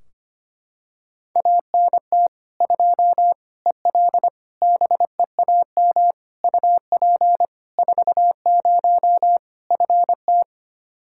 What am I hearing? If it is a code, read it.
Morse code, heard as a keyed tone, about 25 wpm, 700 Hz. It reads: ANT 2 EL BEAM UP 40 FT
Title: Practice QSO - Exchange 2 - Info - 25wpm